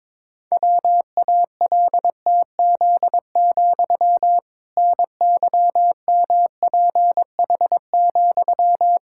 Morse code, heard as a keyed tone, about 22 wpm, 700 Hz. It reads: WALTZ, NYMPH,
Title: Pangram Challenge v2 - 22wpm